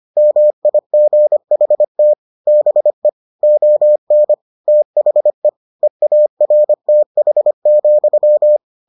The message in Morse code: MIGHT BE ON THE EARTH,

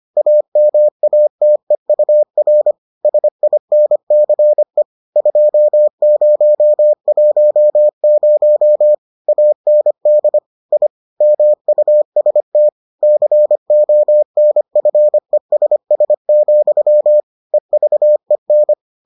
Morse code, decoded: AMATEUR SINCE 2010 AND I MUST CONFESS, EVEN

